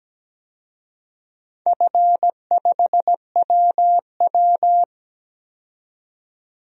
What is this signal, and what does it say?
Morse code, heard as a keyed tone, about 17 wpm, 700 Hz. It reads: F5WW
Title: DXCC-100 Call Signs - 4 Characters - 17wpm